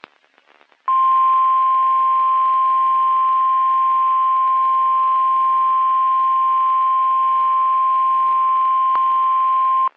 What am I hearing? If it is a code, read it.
Emergency Alert System (SAME) tones (the 1050 Hz attention tone). It is heard from 0.8 to 9.0 s.
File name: eas messages / EAS WXM31 Elmira